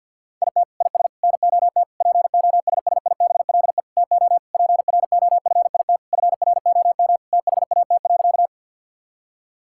Morse code, heard as a keyed tone, about 50 wpm, 700 Hz. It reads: IT IS NOT POSSIBLE TO PROFIT FROM THAT.
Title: Sentences from Top 1000 Words - Review Part 5 - 50wpm